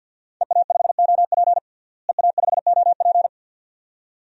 Morse code, heard as a keyed tone, about 51 wpm, 700 Hz. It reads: EA5OP EA5OP
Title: DXCC-100 Call Signs - 5 Characters - Repeated 2x - 50wpm